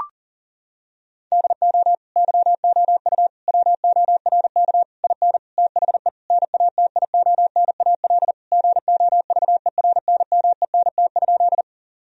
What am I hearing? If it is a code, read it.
Morse code, heard as a keyed tone, about 40 wpm, 700 Hz. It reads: DO YOU WORK IN THE NATIONAL GOVERNMENT?
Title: Sentences from Top 1000 Words - Review Part 4 - 40wpm